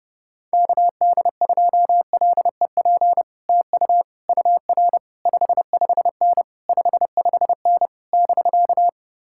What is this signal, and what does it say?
Morse code, heard as a keyed tone, about 30 wpm, 700 Hz. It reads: KD2LEP TU UR 55N 55N <BK>